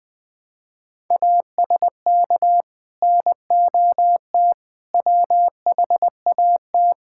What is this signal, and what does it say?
Morse code, heard as a keyed tone, about 20 wpm, 700 Hz. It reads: ASK NOT WHAT
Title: Sentences with 8-Word Phrases - 20wpm